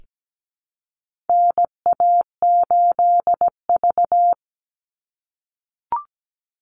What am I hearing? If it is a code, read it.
Morse code, heard as a keyed tone, about 17 wpm, 700 Hz. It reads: NA8V
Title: Call Signs - 17wpm